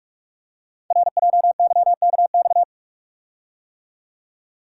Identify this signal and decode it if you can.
Morse code, heard as a keyed tone, about 44 wpm, 700 Hz. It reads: RJYKX
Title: Code Groups - 5 Letters - 45wpm